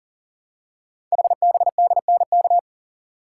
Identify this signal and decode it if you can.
Morse code, heard as a keyed tone, about 42 wpm, 700 Hz. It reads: HBDNK
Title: Code Groups - 5 Letters - 40wpm